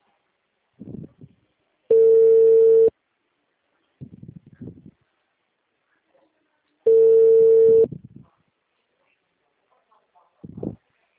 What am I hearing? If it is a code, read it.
Ringback tone (China, 450 Hz), with speech.